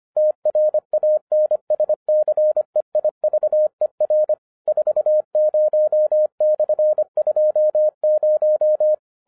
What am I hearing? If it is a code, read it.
Morse code, heard as a keyed tone, about 24 wpm, 610 Hz. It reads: TRANSCEIVER 40/20
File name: 2011-07-mp3 / n24 QRP OTAKU 25wpm